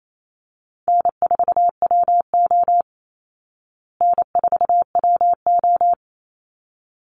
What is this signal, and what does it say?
Morse code, heard as a keyed tone, about 28 wpm, 700 Hz. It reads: N4WO N4WO
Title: Call Signs Repeated 3x - 28wpm